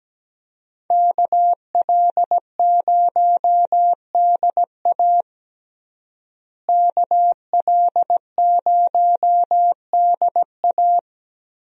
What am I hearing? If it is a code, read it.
Morse code, heard as a keyed tone, about 17 wpm, 700 Hz. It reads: KL0DA KL0DA
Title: USA Call Signs - 5 Characters - Repeated 2x - 17wpm